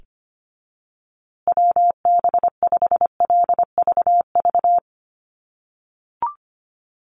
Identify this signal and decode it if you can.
Morse code, heard as a keyed tone, about 25 wpm, 700 Hz. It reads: WB5LVV
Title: USA Call Signs Repeated 2x 25wpm